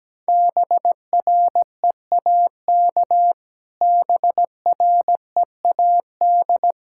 Morse code, heard as a keyed tone, about 17 wpm, 700 Hz. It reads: BREAK BREAD
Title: Sentences from Top 1600 Words 17wpm